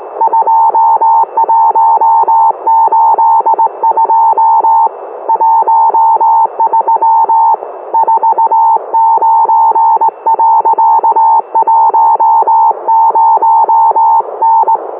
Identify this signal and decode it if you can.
Morse code, heard as a keyed tone, about 18 wpm, 900 Hz. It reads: 2182 13 49.10N